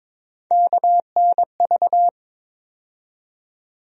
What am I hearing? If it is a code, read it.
Morse code, heard as a keyed tone, about 22 wpm, 700 Hz. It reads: KNV
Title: USA Call Sign Postfixes - 3 Letters 22wpm